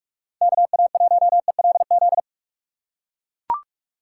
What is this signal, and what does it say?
Morse code, heard as a keyed tone, about 44 wpm, 700 Hz. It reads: KA1ELZ
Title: USA Call Signs - 6 Characters - 4x Character Spacing - 45wpm